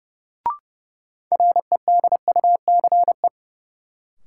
Morse code, heard as a keyed tone, about 30 wpm, 700 Hz. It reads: REDUCE
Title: Sentences from Top 1100 Words - New Word Drill Alt - 30wpm